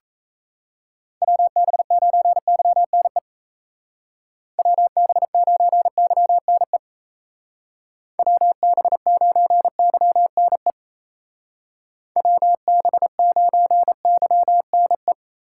Morse code, heard as a keyed tone, about 34 wpm, 700 Hz. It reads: WB9YNE WB9YNE WB9YNE WB9YNE
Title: Speed Racing - USA 2x3 Call Signs 28wpm